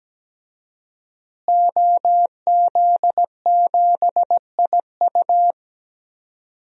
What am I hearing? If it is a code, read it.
Morse code, heard as a keyed tone, about 17 wpm, 700 Hz. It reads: OZ7IU
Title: DXCC-100 Call Signs - 5 Characters - Repeated 2x - 17wpm